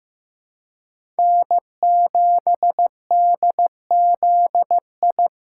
Morse code, heard as a keyed tone, about 15 wpm, 700 Hz. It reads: N7DZI
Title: USA Call Signs - 5 Characters - 2x Character Spacing - 15wpm